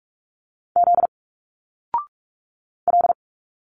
Morse code, heard as a keyed tone, about 46 wpm, 700 Hz. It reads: Z L
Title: Single Letters 45wpm